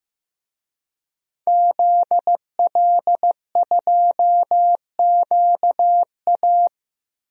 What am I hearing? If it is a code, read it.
Morse code, heard as a keyed tone, about 15 wpm, 700 Hz. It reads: ZL2QA
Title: DXCC-100 Call Signs - 5 Characters - Repeated 3x - 15wpm